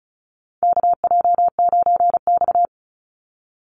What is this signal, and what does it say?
Morse code, heard as a keyed tone, about 34 wpm, 700 Hz. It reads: KJ9X
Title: NA SOTA Chaser Callsigns 35wpm